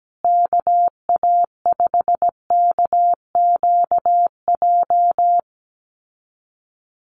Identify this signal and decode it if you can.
Morse code, heard as a keyed tone, about 17 wpm, 700 Hz. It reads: KA5KQJ
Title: USA Call Signs Repeated 2x 17wpm